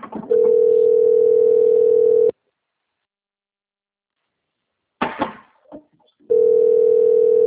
Ringback tone (North America), 440 + 480 Hz.